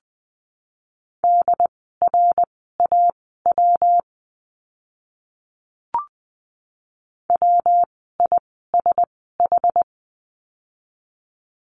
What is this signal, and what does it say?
Morse code, heard as a keyed tone, about 20 wpm, spaced as for 14 wpm, 700 Hz. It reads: DRAW WISH
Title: 115 Lesson 19f - Letter w - Words 20wpm